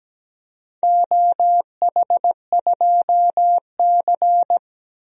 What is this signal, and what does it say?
Morse code, heard as a keyed tone, about 17 wpm, 700 Hz. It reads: OH2C
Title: DXCC-100 Call Signs - 4 Characters - 17wpm